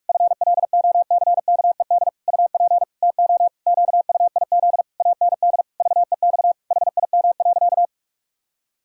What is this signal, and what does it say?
Morse code, heard as a keyed tone, about 46 wpm, 700 Hz. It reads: FLOCKED UP TO QUIZ AND VEX HIM.
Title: Pangram Challenge v2 - 45wpm